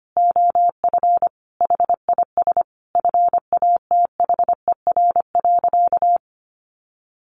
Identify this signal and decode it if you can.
Morse code, heard as a keyed tone, about 25 wpm, 700 Hz. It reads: OF HIS FATHER.
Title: Sentences from Top 1500 Words 25wpm